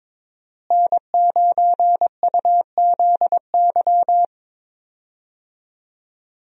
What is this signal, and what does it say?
Morse code, heard as a keyed tone, about 22 wpm, 700 Hz. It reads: N9UZY
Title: USA Call Signs - 5 Characters - Repeated 3x - 22wpm